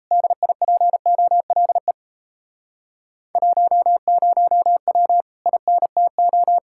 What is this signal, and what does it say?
Morse code, heard as a keyed tone, about 37 wpm, 700 Hz. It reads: DIPOLE 10W INTO